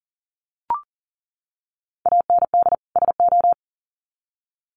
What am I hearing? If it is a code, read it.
Morse code, heard as a keyed tone, about 40 wpm, 700 Hz. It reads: AND SO
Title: Sets of 2 Words 40wpm